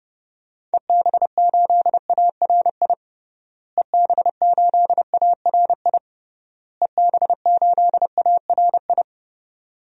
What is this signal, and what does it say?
Morse code, heard as a keyed tone, about 32 wpm, 700 Hz. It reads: EB8ARI EB8ARI EB8ARI
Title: DXCC-100 Call Signs - 6 Characters - Repeated 3x - 30wpm